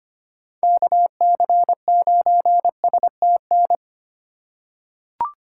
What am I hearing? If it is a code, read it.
Morse code, heard as a keyed tone, about 25 wpm, 700 Hz. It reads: KC9STN
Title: USA Call Signs - 3x Character Spacing - 25wpm